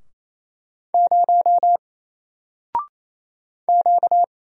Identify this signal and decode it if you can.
Morse code, heard as a keyed tone, about 27 wpm, 700 Hz. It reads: 0 Q
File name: Single-Letter-Number-28wpm